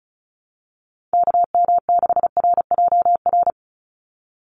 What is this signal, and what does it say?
Morse code, heard as a keyed tone, about 34 wpm, 700 Hz. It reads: KM6RJR
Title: USA Call Signs 35wpm